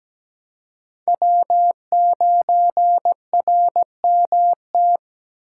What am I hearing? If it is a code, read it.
Morse code, heard as a keyed tone, about 20 wpm, 700 Hz. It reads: W9RMT
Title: Speed Racing - USA 1x3 Call Signs 17wpm